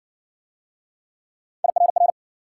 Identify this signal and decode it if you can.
Morse code, heard as a keyed tone, about 52 wpm, 700 Hz. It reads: IRR